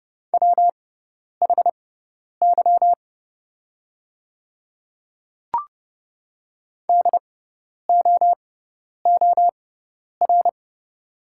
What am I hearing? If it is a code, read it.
Morse code, heard as a keyed tone, about 30 wpm, spaced as for 10 wpm, 700 Hz. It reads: WHY DOOR
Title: 6x Character Spacing - Top 300 Words - 30wpm